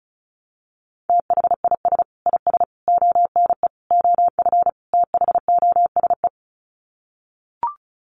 Morse code, heard as a keyed tone, about 34 wpm, 700 Hz. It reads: THIS IS ONE OF THOSE
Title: Sets of 5 Words Encore2 35wpm